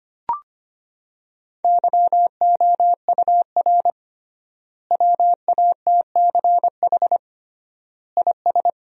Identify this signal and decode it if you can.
Morse code, heard as a keyed tone, about 25 wpm, 700 Hz. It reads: YOUR WATCH IS
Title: 3x Word-Spacing - Sentences from Top 400 Words - 25wpm